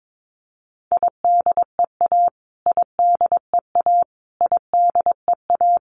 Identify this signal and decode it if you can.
Morse code, heard as a keyed tone, about 22 wpm, 700 Hz. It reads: IDEA IDEA IDEA
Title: Top 500 Words Repeated 22wpm